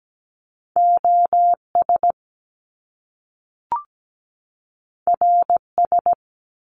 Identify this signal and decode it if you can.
Morse code, heard as a keyed tone, about 17 wpm, 700 Hz. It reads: OS RS
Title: Top Two Letter Combinations in Words 17wpm